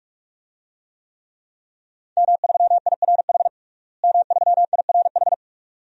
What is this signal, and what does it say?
Morse code, heard as a keyed tone, about 41 wpm, 700 Hz. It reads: M3IRH M3IRH
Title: DXCC-100 Call Signs - 5 Characters - Repeated 2x - 45wpm